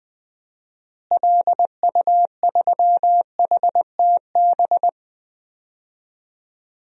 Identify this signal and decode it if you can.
Morse code, heard as a keyed tone, about 20 wpm, 700 Hz. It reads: LU3HTB
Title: DXCC-100 Call Signs - 6 Characters - Repeated 3x - 20wpm